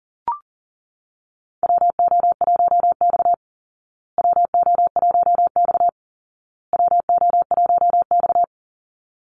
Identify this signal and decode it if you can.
Morse code, heard as a keyed tone, about 40 wpm, 700 Hz. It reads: WO1X WO1X WO1X
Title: Call Signs Repeated 2x - 40wpm